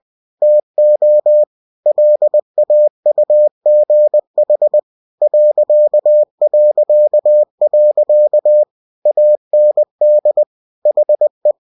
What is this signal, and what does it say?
Morse code, heard as a keyed tone, about 20 wpm, 600 Hz. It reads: TO LAUGH ... AND HE
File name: cw-books / winnie-the-pooh / winnie-20wpm-page-0043